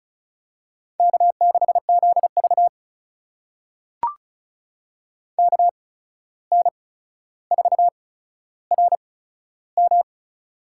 Morse code, heard as a keyed tone, about 36 wpm, spaced as for 14 wpm, 700 Hz. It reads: K6ZV KN4RM
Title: USA Call Signs - 8x Character Spacing - 35wpm